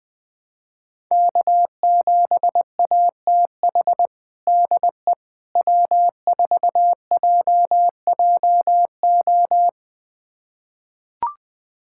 Morse code, heard as a keyed tone, about 21 wpm, 700 Hz. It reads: K7ATH DE W4JJO
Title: USA Callsign DE Callsign - 5 Characters - 20wpm